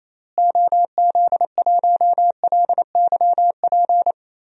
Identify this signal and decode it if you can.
Morse code, heard as a keyed tone, about 28 wpm, 700 Hz. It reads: OZ1LYP
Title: DXCC-100 Call Signs - 6 Characters - Repeated 3x - 28wpm